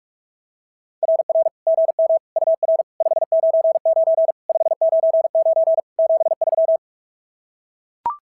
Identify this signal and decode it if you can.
Morse code, heard as a keyed tone, about 44 wpm, 660 Hz. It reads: RR GM UR 599 599 73
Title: SOTA QSO Practice 45wpm